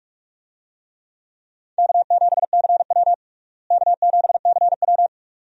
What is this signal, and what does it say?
Morse code, heard as a keyed tone, about 46 wpm, 700 Hz. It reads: K7CW K7CW
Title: USA Call Signs - 4 Characters - Repeated 2x - 45wpm